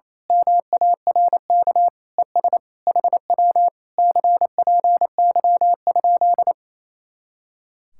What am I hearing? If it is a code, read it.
Morse code, heard as a keyed tone, about 26 wpm, 700 Hz. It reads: MARK ES HW CPY?
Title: Speed-Racing Common QSO Phrases - 28wpm